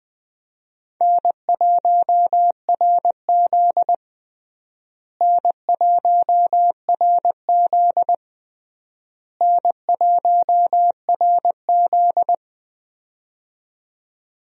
Morse code, heard as a keyed tone, about 20 wpm, 700 Hz. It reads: N1RZ N1RZ N1RZ
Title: USA Call Signs - 4 Characters - Repeated 3x - 20wpm